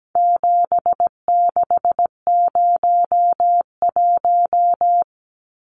Morse code, heard as a keyed tone, about 17 wpm, 700 Hz. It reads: 7601
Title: Numbers - 4-Digit 17wpm